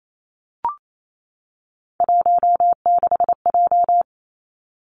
Morse code, heard as a keyed tone, about 28 wpm, 700 Hz. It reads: 16J
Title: Three Challenging Letter-Number Combinations 28wpm